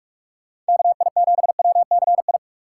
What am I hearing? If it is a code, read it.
Morse code, heard as a keyed tone, about 57 wpm, 700 Hz. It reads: KI7WCI